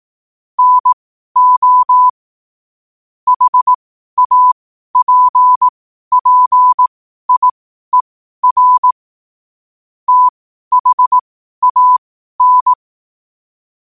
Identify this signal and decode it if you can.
Morse code, heard as a keyed tone, about 18 wpm, spaced as for 12 wpm, 1000 Hz. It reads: NO HAPPIER THAN
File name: qotdfiles / 12wpm